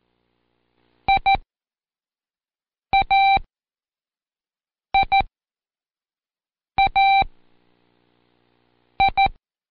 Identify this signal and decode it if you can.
Morse code, heard as a keyed tone, about 14 wpm, spaced as for 5 wpm, 780 Hz. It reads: IAIAI
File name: Code Quick / AudioDownld1 / Seg42-Day5-Intro F